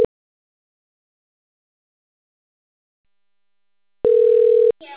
Ringback tone (North America), 440 + 480 Hz.